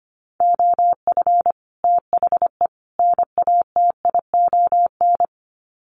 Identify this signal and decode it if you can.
Morse code, heard as a keyed tone, about 25 wpm, 700 Hz. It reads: OF THE NATION
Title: Sets of 3 Words 25wpm